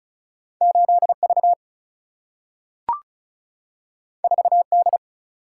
Morse code, heard as a keyed tone, about 36 wpm, 700 Hz. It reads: 8V 4D
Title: Single-Letter Single-Number 35wpm